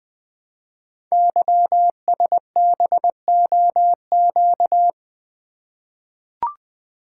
Morse code, heard as a keyed tone, about 20 wpm, 700 Hz. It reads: YSBOQ
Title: Code Groups - 5 Letters - 20wpm